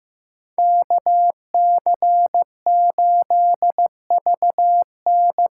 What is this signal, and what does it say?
Morse code, heard as a keyed tone, about 15 wpm, 700 Hz. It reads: KC8VN
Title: USA Call Signs - 6 Characters - 6x Character Spacing - 15wpm